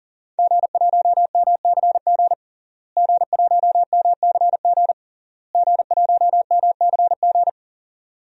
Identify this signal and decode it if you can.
Morse code, heard as a keyed tone, about 40 wpm, 700 Hz. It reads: G1MCG G1MCG G1MCG
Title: DXCC-100 Call Signs - 5 Characters - Repeated 3x - 40wpm